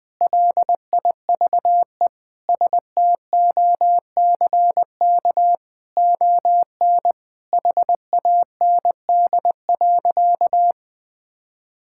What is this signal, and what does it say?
Morse code, heard as a keyed tone, about 20 wpm, 700 Hz. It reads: LIVE STOCK ON HAND.
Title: Sentences from Top 1000 Words - Review Part 5 - 20wpm